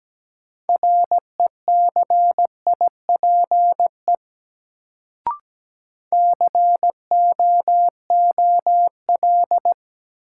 Morse code, heard as a keyed tone, about 17 wpm, 700 Hz. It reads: RECIPE COOL
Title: Sentences from Top 1200 Words - New Word Drill - 17wpm